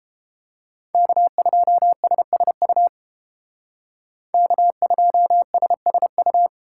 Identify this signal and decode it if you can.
Morse code, heard as a keyed tone, about 30 wpm, 700 Hz. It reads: K2SSU K2SSU